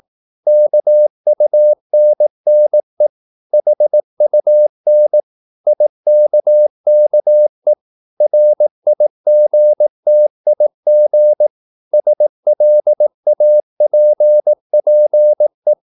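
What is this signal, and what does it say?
Morse code, heard as a keyed tone, about 18 wpm, 600 Hz. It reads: KUNNE HUN IKKE RIGTIG SLAPPE
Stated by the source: Slow (QRS) Morse practice at 18 wpm, CW med Gnister.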